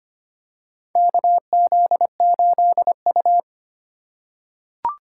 Morse code, heard as a keyed tone, about 25 wpm, 700 Hz. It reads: KZ8U
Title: USA Call Signs - 4 Characters - 2x Character Spacing - 25wpm